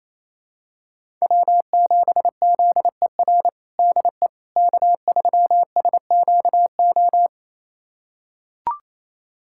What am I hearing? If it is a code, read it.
Morse code, heard as a keyed tone, about 28 wpm, 700 Hz. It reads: W7ZER DE K3SQO
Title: USA Callsign DE Callsign - 5 Characters - 28wpm